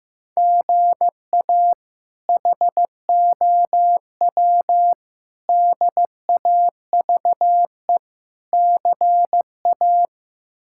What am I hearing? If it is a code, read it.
Morse code, heard as a keyed tone, about 15 wpm, 700 Hz. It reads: GA HOW DAVE CA
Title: K1USN SST QSO Practice - 15wpm